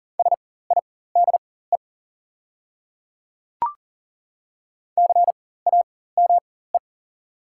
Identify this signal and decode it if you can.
Morse code, heard as a keyed tone, about 40 wpm, spaced as for 19 wpm, 700 Hz. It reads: SIDE CAME
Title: 4x Character Spacing - Top 200 Words - 40wpm